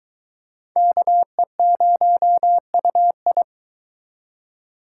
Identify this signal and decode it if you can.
Morse code, heard as a keyed tone, about 22 wpm, 700 Hz. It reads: KE0UI